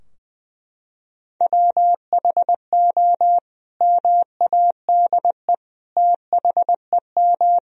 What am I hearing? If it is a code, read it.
Morse code, heard as a keyed tone, about 20 wpm, 700 Hz. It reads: WHO MADE THEM
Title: Sentences from Top 1000 Words - Review Part 2 - 20wpm